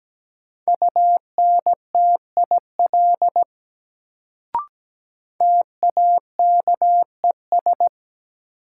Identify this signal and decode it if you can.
Morse code, heard as a keyed tone, about 17 wpm, 700 Hz. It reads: UNTIL TAKES
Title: Sets of 5 Words Encore - New Word Drill - 17wpm